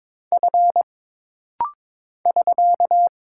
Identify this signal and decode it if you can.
Morse code, heard as a keyed tone, about 22 wpm, 700 Hz. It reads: F <SK>
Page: This rapid-fire timing is geared to push your ICR (Instant Character Recognition) skills.